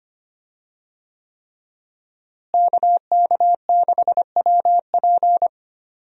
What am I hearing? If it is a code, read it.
Morse code, heard as a keyed tone, about 25 wpm, 700 Hz. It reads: KK6WP
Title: USA Call Signs - 5 Characters - Repeated 3x - 25wpm